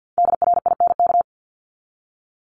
Morse code, heard as a keyed tone, about 50 wpm, 700 Hz. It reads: DRINK
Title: Top 1251-1500 Words 50wpm